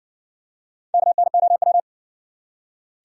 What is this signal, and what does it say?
Morse code, heard as a keyed tone, about 58 wpm, 700 Hz. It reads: KNOW